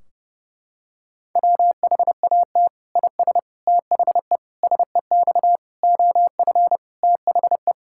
Morse code, heard as a keyed tone, about 30 wpm, 700 Hz. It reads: WHAT IS THE SEX OF THE
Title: Sentences from Top 1600 Words 30wpm